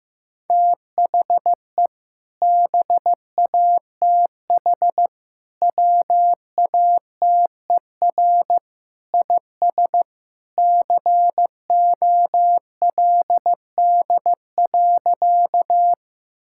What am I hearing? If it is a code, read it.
Morse code, heard as a keyed tone, about 15 wpm, 700 Hz. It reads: THE BATH WATER IS COLD.
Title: Sentences from Top 1600 Words 15wpm